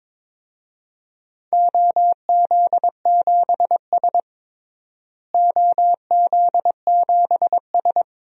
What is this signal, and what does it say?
Morse code, heard as a keyed tone, about 22 wpm, 700 Hz. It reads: OZ7S OZ7S
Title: DXCC-100 Call Signs - 4 Characters - Repeated 2x - 22wpm